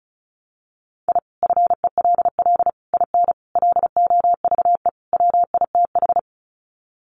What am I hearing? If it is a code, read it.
Morse code, heard as a keyed tone, about 34 wpm, 700 Hz. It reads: I FELL IN LOVE WITH
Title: Sets of 5 Words Encore 35wpm